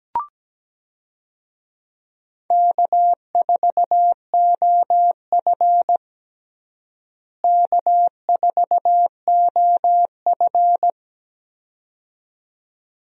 Morse code, heard as a keyed tone, about 17 wpm, 700 Hz. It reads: K4OF K4OF
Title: USA Call Signs - 4 Characters - Repeated 2x - 17wpm